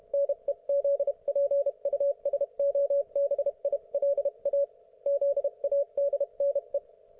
Morse code, heard as a keyed tone, about 30 wpm, 560 Hz. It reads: NEZPUSOBILA ZADNE